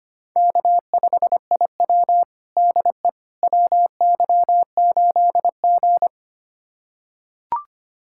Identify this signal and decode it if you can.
Morse code, heard as a keyed tone, about 25 wpm, 700 Hz. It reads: K5IW DE WY8G
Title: USA Callsign DE Callsign - 4 Characters - 25wpm